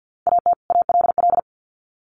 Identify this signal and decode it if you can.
Morse code, heard as a keyed tone, about 48 wpm, 700 Hz. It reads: AT ALL
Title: Sets of 2 Words 50wpm